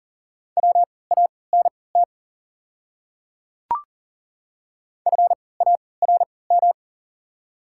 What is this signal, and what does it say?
Morse code, heard as a keyed tone, about 40 wpm, spaced as for 23 wpm, 690 Hz. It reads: WANT FARM
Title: 3x Character Spacing - Top 500 Words - 40wpm